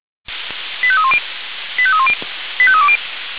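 Five-tone selective call (ZVEI), heard as quick runs of five tones, tones of about 70 ms. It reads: 85310 85310 85320
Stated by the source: ZVEI five-tone selective call 85320, received on SDR.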